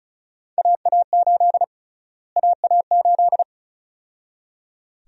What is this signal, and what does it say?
Morse code, heard as a keyed tone, about 36 wpm, 700 Hz. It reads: AA8 AA8
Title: USA Call Sign Prefixes - Repeated 2x - 35wpm